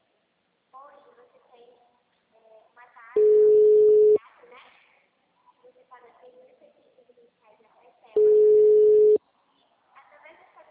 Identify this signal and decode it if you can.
Ringback tone (Europe, 425 Hz), with speech.